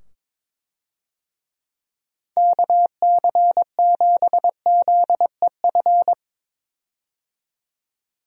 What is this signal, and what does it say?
Morse code, heard as a keyed tone, about 22 wpm, 700 Hz. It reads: KC7ZEF
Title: USA Call Signs - 6 Characters - 22wpm